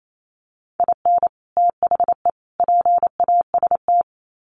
Morse code, heard as a keyed tone, about 28 wpm, 700 Hz. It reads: IN THE PAST
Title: Sets of 3 Words 28wpm